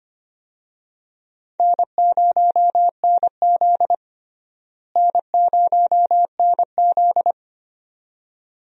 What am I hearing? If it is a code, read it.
Morse code, heard as a keyed tone, about 25 wpm, 700 Hz. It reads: N0NZ N0NZ
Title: USA Call Signs - 4 Characters - Repeated 2x - 25wpm